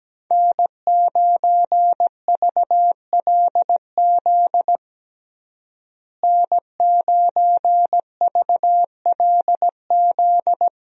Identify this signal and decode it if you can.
Morse code, heard as a keyed tone, about 17 wpm, 700 Hz. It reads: N9VLZ N9VLZ
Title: USA Call Signs - 5 Characters - Repeated 2x - 17wpm